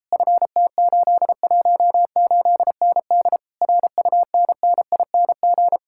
Morse code, heard as a keyed tone, about 34 wpm, 700 Hz. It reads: FT818ND RUNNING